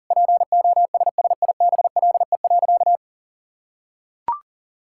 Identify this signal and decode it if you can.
Morse code, heard as a keyed tone, about 40 wpm, 700 Hz. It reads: POSSIBLE.
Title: Sentences from Top 1000 Words - Review Part 6 - 40wpm